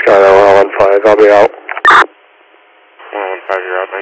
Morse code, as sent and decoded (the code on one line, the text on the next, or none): none